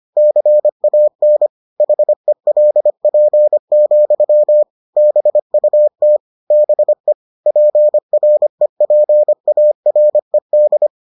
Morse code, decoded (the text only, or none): CAN HELP, BUT BE PREPARED